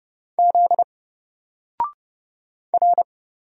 Morse code, heard as a keyed tone, about 30 wpm, 700 Hz. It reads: Z R
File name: Single-Letter-Number-30wpm